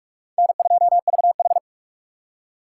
Morse code, heard as a keyed tone, about 42 wpm, 700 Hz. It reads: N2VH
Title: USA Call Signs - 4 Characters - 4x Character Spacing - 45wpm